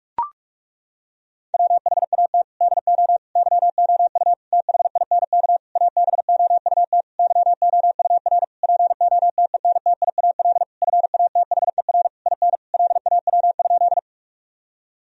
Morse code, heard as a keyed tone, about 41 wpm, 700 Hz. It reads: WHAT DO YOU THINK ABOUT YOUR POTENTIAL FATHER IN LAW?
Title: Sentences from Top 1000 Words - Review Part 4 - 45wpm